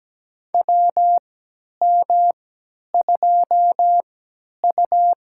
Morse code, heard as a keyed tone, about 17 wpm, spaced as for 10 wpm, 700 Hz. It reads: WM2U
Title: USA Call Signs - 4 Characters - 3x Character Spacing - 17wpm